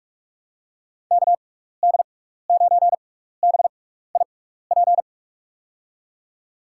Morse code, heard as a keyed tone, about 43 wpm, spaced as for 15 wpm, 700 Hz. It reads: KD9BIP
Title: USA Call Signs - 6 Characters - 6x Character Spacing - 45wpm